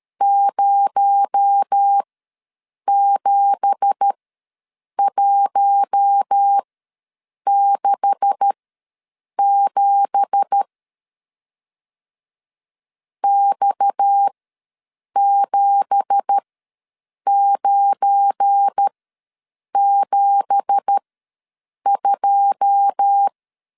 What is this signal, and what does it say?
Morse code, heard as a keyed tone, about 13 wpm, spaced as for 7 wpm, 800 Hz. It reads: 07167 X7972
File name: LEZIONI / 9 2867 ws 12wpm 3431